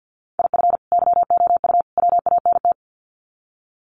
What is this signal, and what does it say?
Morse code, heard as a keyed tone, about 50 wpm, 700 Hz. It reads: IF YOU WANT
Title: Sets of 3 Words 50wpm